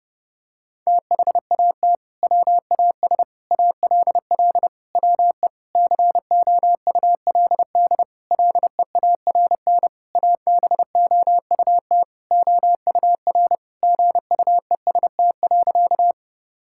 Morse code, heard as a keyed tone, about 32 wpm, 700 Hz. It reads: THAT WAS ALL WE COULD LEARN ABOUT OUR GUEST.
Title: Sentences from Top 1600 Words 30wpm